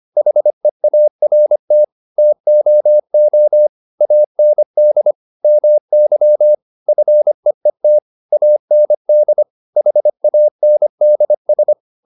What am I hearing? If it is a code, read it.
Morse code, heard as a keyed tone, about 25 wpm, 600 Hz. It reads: HEART TOO AND MY FEET AND HANDS
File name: the old man and the sea part 1 25wpm